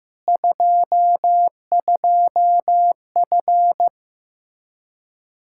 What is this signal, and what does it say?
Morse code, heard as a keyed tone, about 16 wpm, 700 Hz. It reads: 22F
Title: Practice QSO - Exchange 2 - Info - 15wpm